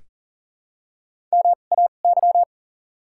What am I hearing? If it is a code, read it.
Morse code, heard as a keyed tone, about 40 wpm, spaced as for 29 wpm, 700 Hz. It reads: MAY
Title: 2x Character Spacing - Top 200 Words - 40wpm